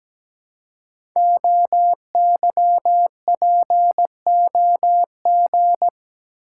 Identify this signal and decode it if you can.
Morse code, heard as a keyed tone, about 17 wpm, 700 Hz. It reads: OYPOG
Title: Code Groups - 5 Letters - 17wpm